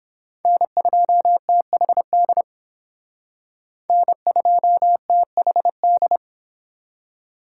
Morse code, heard as a keyed tone, about 28 wpm, 700 Hz. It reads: N2THD N2THD